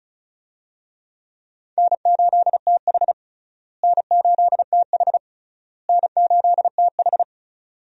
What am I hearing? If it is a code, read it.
Morse code, heard as a keyed tone, about 36 wpm, 700 Hz. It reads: N8TH N8TH N8TH
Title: USA Call Signs - 4 Characters - Repeated 3x - 35wpm